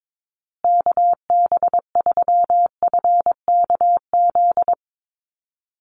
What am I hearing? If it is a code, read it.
Morse code, heard as a keyed tone, about 22 wpm, 700 Hz. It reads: KB3FKZ
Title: USA Call Signs Repeated 3x 22wpm